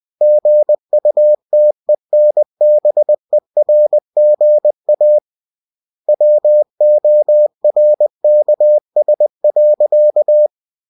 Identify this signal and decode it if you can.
Morse code, heard as a keyed tone, about 20 wpm, 600 Hz. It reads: GUTENBERGA WORKS.
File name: cw-books / winnie-the-pooh / winnie-20wpm-page-0117